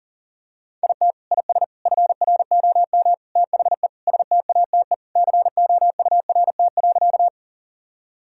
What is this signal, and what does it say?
Morse code, heard as a keyed tone, about 37 wpm, 700 Hz. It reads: IT IS FROM THE STATE COURT.
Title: Sentences from Top 1600 Words 40wpm